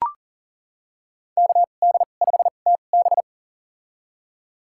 Morse code, heard as a keyed tone, about 40 wpm, spaced as for 29 wpm, 700 Hz. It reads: KD5TB
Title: USA Call Signs - 5 Characters - 2x Character Spacing - 40wpm